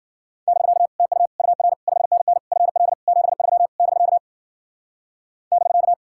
Morse code, heard as a keyed tone, about 52 wpm, 700 Hz. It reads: <BK> TU UR 5NN FL 73 <BK> <BK>